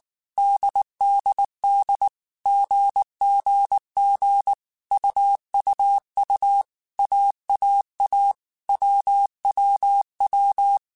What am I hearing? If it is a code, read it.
Morse code, heard as a keyed tone, about 19 wpm, 800 Hz. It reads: DDD GGG UUU AAA WWW